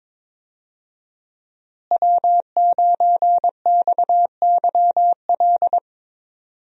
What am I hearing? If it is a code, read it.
Morse code, heard as a keyed tone, about 22 wpm, 700 Hz. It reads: W9XYL
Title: USA Call Signs - 5 Characters - Repeated 2x - 22wpm